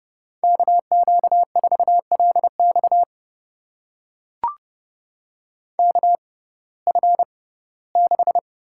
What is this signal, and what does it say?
Morse code, heard as a keyed tone, about 30 wpm, spaced as for 18 wpm, 700 Hz. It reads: KQ4LX KF6
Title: USA Call Signs - 5 Characters - 6x Character Spacing - 30wpm